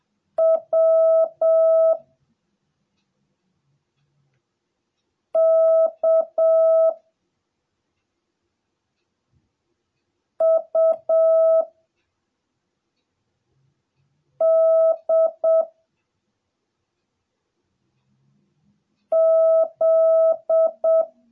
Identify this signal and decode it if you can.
Morse code, heard as a keyed tone, about 7 wpm, 650 Hz. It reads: W K U D Z